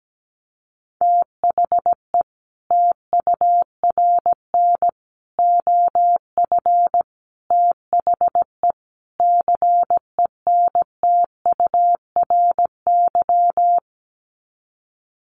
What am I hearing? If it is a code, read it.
Morse code, heard as a keyed tone, about 17 wpm, 700 Hz. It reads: THE TURN OF THE CENTURY
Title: Sets of 5 Words 17wpm